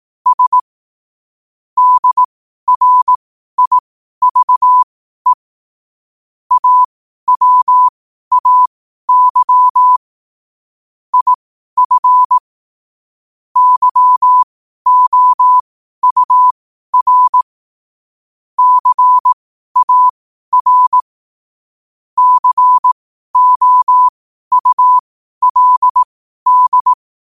Morse code, heard as a keyed tone, about 18 wpm, spaced as for 12 wpm, 1000 Hz. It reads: S DRIVE AWAY IF YOUR CAR COULD
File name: qotdfiles / 12wpm